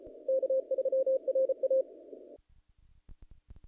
Morse code, heard as a keyed tone, about 36 wpm, 500 Hz. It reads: K3RA
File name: contests / naqp-cw-k5tr-op-k5pi- -1900